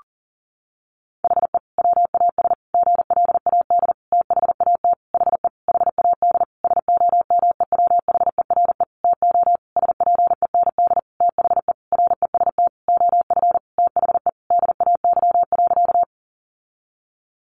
Morse code, heard as a keyed tone, about 40 wpm, 700 Hz. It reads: HE WAS GLAD THAT HE HAD SOMEWHERE TO SPEND THE REST OF THE DAY.
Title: Sentences from Top 1400 Words 40wpm